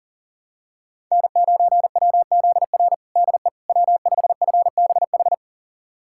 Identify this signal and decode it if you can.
Morse code, heard as a keyed tone, about 40 wpm, 700 Hz. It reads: N9WZR DE W5FBH
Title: USA Callsign DE Callsign - 5 Characters - 40wpm